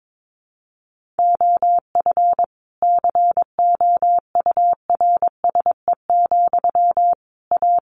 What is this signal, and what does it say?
Morse code, heard as a keyed tone, about 22 wpm, 700 Hz. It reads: OF COURSE, A
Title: Sentences from Top 1400 Words 22wpm